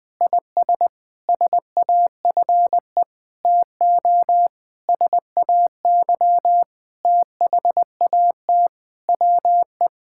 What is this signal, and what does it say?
Morse code, heard as a keyed tone, about 20 wpm, 700 Hz. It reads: IS SAFE TO SAY THAT WE
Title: Sentences with 8-Word Phrases - Spoken Beginning - 20wpm